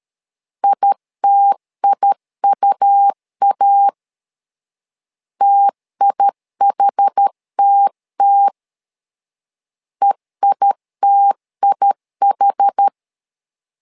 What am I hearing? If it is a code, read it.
Morse code, heard as a keyed tone, about 12 wpm, 800 Hz. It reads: ITIUA TIHTT EITIH
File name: LEZIONI / 2 UHDN / 2 UH 12wpm 1439